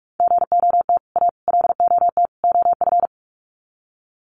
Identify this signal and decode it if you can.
Morse code, heard as a keyed tone, about 44 wpm, 700 Hz. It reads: GOT A LOT OF
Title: Sets of 4 Words 45wpm